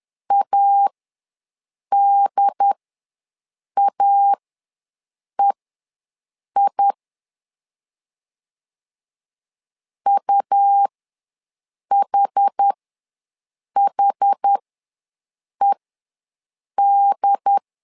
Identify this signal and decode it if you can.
Morse code, heard as a keyed tone, about 10 wpm, spaced as for 6 wpm, 800 Hz. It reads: ADAEI UHHED
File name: LEZIONI / 2 UHDN / 2 UHD ws 10wpm 2058